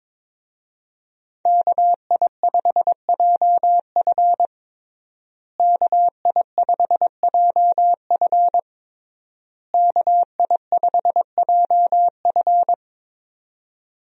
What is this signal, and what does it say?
Morse code, heard as a keyed tone, about 22 wpm, 700 Hz. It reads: KI5JF KI5JF KI5JF
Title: USA Call Signs - 5 Characters - Repeated 3x - 22wpm